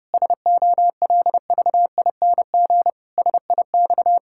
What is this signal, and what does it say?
Morse code, heard as a keyed tone, about 30 wpm, 700 Hz. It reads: SOLVING SIX
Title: Pangram Challenge v2 - 30wpm